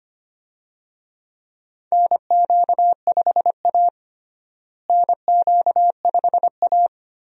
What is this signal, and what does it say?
Morse code, heard as a keyed tone, about 25 wpm, 700 Hz. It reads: NQ5A NQ5A
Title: USA Call Signs - 4 Characters - Repeated 2x - 25wpm